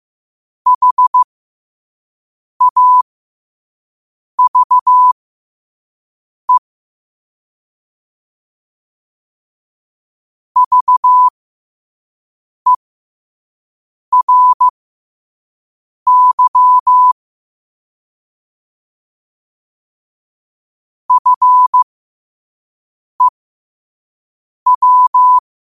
Morse code, heard as a keyed tone, about 15 wpm, spaced as for 5 wpm, 1000 Hz. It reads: HAVE VERY FEW
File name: qotdfiles / 5wpm